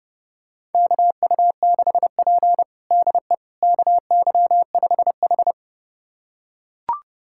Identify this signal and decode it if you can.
Morse code, heard as a keyed tone, about 30 wpm, 700 Hz. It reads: KU6P DE KY5H
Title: USA Callsign DE Callsign - 4 Characters - 30wpm